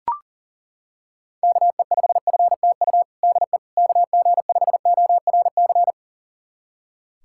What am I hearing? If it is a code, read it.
Morse code, heard as a keyed tone, about 40 wpm, 700 Hz. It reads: KE5FTU DE KG5ORC
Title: USA Callsign DE Callsign 40wpm